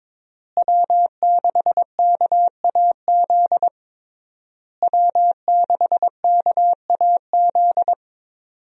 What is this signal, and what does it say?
Morse code, heard as a keyed tone, about 22 wpm, 700 Hz. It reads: W6KAZ W6KAZ
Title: USA Call Signs - 5 Characters - Repeated 3x - 22wpm